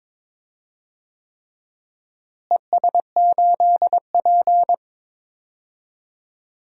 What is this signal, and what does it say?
Morse code, heard as a keyed tone, about 22 wpm, 700 Hz. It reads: ES8P
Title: DXCC-100 Call Signs - 4 Characters - Repeated 3x - 22wpm